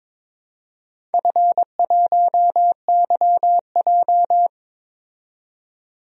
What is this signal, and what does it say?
Morse code, heard as a keyed tone, about 22 wpm, 700 Hz. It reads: F1YJ
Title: DXCC-100 Call Signs - 4 Characters - 22wpm